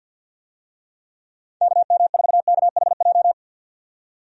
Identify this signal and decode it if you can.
Morse code, heard as a keyed tone, about 53 wpm, 700 Hz. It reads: KM4KHJ
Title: USA Call Signs - 6 Characters - 50wpm